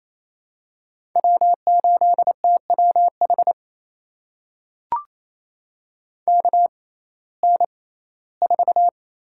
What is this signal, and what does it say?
Morse code, heard as a keyed tone, about 28 wpm, spaced as for 17 wpm, 700 Hz. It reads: W8TWH KN4
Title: USA Call Signs - 5 Characters - 6x Character Spacing - 28wpm